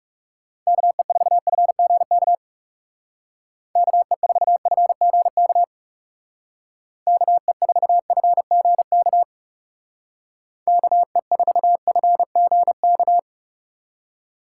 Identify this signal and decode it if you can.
Morse code, heard as a keyed tone, about 36 wpm, 700 Hz. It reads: KE4FGK KE4FGK KE4FGK KE4FGK
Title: Speed Racing - USA 2x3 Call Signs 30wpm